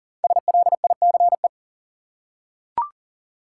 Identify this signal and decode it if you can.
Morse code, heard as a keyed tone, about 39 wpm, 700 Hz. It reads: SLICE
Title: Five Letter Words - 40wpm